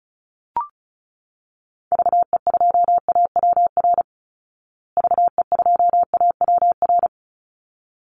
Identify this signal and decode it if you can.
Morse code, heard as a keyed tone, about 34 wpm, 700 Hz. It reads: VE2AWR VE2AWR
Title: Call Signs Repeated 3x - 35wpm